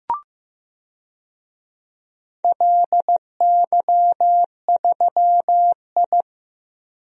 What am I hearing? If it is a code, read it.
Morse code, heard as a keyed tone, about 15 wpm, 700 Hz. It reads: LY3I
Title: DXCC-100 Call Signs - 4 Characters - 15wpm